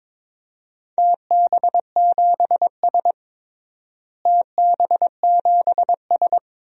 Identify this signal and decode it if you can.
Morse code, heard as a keyed tone, about 22 wpm, 700 Hz. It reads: TB7S TB7S
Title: DXCC-100 Call Signs - 4 Characters - Repeated 3x - 22wpm